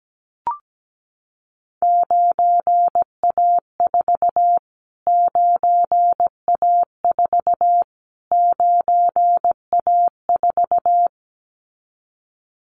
Morse code, heard as a keyed tone, about 17 wpm, 700 Hz. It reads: 9A4 9A4 9A4
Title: Callsign Prefixes for Top 25 DXCC - Repeated 3x - 17wpm